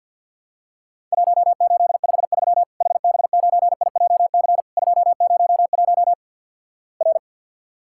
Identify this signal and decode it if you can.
Morse code, heard as a keyed tone, about 52 wpm, 700 Hz. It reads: 1853 HB9IJC 201 R
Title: Worked All Europe DX Contest - QTC Exchange with Repeats - 50wpm